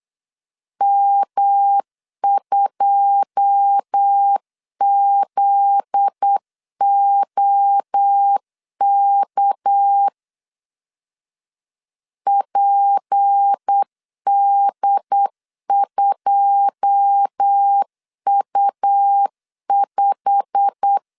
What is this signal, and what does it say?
Morse code, heard as a keyed tone, about 8 wpm, 800 Hz. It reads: M2ZOK PD2U5
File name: LEZIONI / 9 2 8wpm 3437